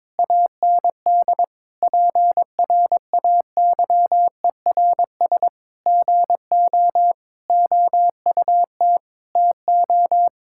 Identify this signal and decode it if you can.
Morse code, heard as a keyed tone, about 22 wpm, 700 Hz. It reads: AND PRAYERS GO OUT TO
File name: practice / 22wpm / Sentences with 8-Word Phrases - 22wpm